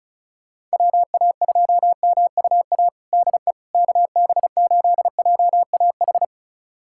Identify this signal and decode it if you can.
Morse code, heard as a keyed tone, about 36 wpm, 700 Hz. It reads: WA2MUA DE KB8JAH
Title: USA Callsign DE Callsign 35wpm